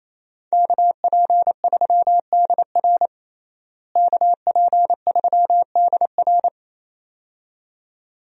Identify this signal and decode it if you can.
Morse code, heard as a keyed tone, about 28 wpm, 700 Hz. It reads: KP3DR KP3DR
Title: USA Call Signs - 5 Characters - Repeated 2x - 28wpm